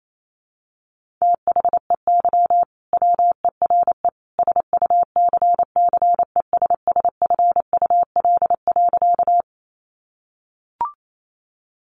Morse code, heard as a keyed tone, about 28 wpm, 700 Hz. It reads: THEY WERE SUCCESSFUL.
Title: Sentences from Top 900 Words 28wpm